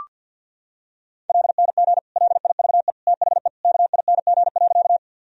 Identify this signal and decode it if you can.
Morse code, heard as a keyed tone, about 46 wpm, 700 Hz. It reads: LNG LIVE THE KING.
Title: Sentences from Top 400 Words and Abbreviations 50wpm